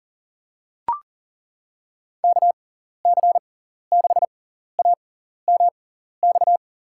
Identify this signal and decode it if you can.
Morse code, heard as a keyed tone, about 38 wpm, spaced as for 14 wpm, 700 Hz. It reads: KC6AMX
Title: USA Call Signs - 6 Characters - 6x Character Spacing - 40wpm